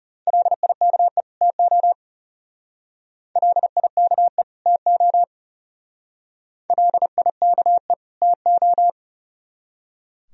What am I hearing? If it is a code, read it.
Morse code, heard as a keyed tone, about 34 wpm, 700 Hz. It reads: LIKE TO LIKE TO LIKE TO
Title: Speed-Racing - Sets of 2 Words 30wpm